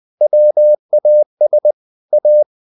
Morse code, heard as a keyed tone, about 20 wpm, 600 Hz. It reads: WAS A
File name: the old man and the sea / the old man and the sea part 1 20wpm